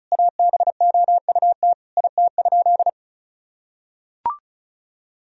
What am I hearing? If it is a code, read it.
Morse code, heard as a keyed tone, about 34 wpm, 700 Hz. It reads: ABOUT IT?
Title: Sentences from Top 1000 Words - Review Part 4 - 35wpm